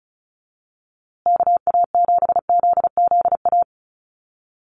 Morse code, heard as a keyed tone, about 34 wpm, 700 Hz. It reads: KA7ZZA
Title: USA Call Signs Repeated 3x 35wpm